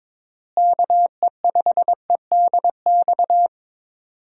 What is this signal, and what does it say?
Morse code, heard as a keyed tone, about 22 wpm, 700 Hz. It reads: KE5EDX
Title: USA Call Signs - 6 Characters - Repeated 2x - 22wpm